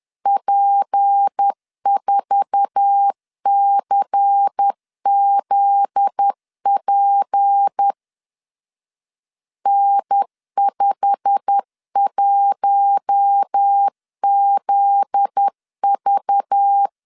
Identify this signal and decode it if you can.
Morse code, heard as a keyed tone, about 10 wpm, 800 Hz. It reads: P4CZP N51ZV
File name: LEZIONI / 8 YJ45 / 8 YJ45 10wpm 2549